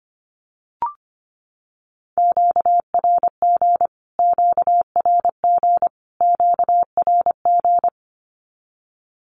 Morse code, heard as a keyed tone, about 25 wpm, 700 Hz. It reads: QRG QRG QRG
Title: Q-Codes Repeated 3x 25wpm